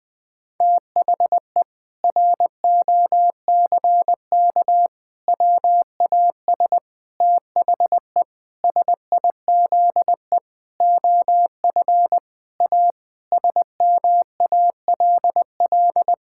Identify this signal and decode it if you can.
Morse code, heard as a keyed tone, about 20 wpm, 700 Hz. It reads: THE ROCK WAS THE SIZE OF A SMALL
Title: Sentences with 4-Word Phrases - 20wpm